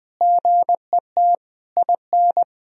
Morse code, heard as a keyed tone, about 21 wpm, 700 Hz. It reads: GET IN
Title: Sentences from Top 1000 Words - Review Part 4 - 20wpm